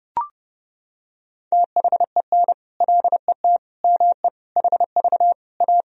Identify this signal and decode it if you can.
Morse code, heard as a keyed tone, about 30 wpm, 700 Hz. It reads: THEN LET ME HV A
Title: Sentences from Top 300 Words and Abbreviations 30wpm